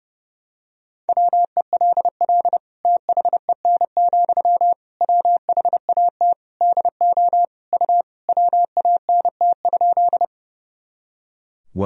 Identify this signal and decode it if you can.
Morse code, heard as a keyed tone, about 32 wpm, 700 Hz. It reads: WELL THEN, WHAT DO U WANT?
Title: Sentences from Top 200 Words and Abbreviations 30wpm